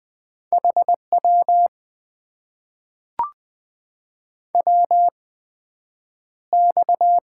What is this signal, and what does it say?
Morse code, heard as a keyed tone, about 20 wpm, 700 Hz. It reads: H W W X
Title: Top 90 QSO Elements - 8x Character Spacing - 20wpm